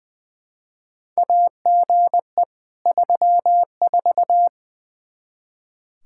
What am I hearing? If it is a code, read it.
Morse code, heard as a keyed tone, about 20 wpm, 700 Hz. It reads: AGE 34
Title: K1USN SST QSO Practice 20wpm